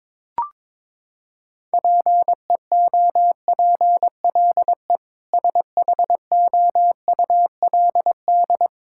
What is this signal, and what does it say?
Morse code, heard as a keyed tone, about 22 wpm, 700 Hz. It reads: PEOPLE SHOULD
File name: practice / 22wpm / Sentences with 8-Word Phrases - 22wpm